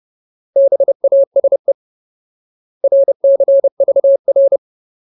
Morse code, heard as a keyed tone, about 30 wpm, 550 Hz. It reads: BASE RCVR
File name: Ultimate Word List / 30WPM Ultimate